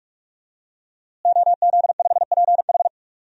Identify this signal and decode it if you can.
Morse code, heard as a keyed tone, about 46 wpm, 700 Hz. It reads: OZ5PH
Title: DXCC-100 Call Signs - 5 Characters - Repeated 3x - 45wpm